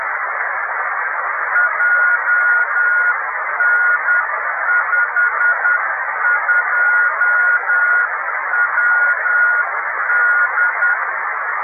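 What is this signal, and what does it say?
Morse code, heard as a keyed tone, about 10 wpm, 1440 Hz, with radio noise. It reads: JN52WD
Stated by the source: Italian amateur beacon IW0HK on 28322 kHz.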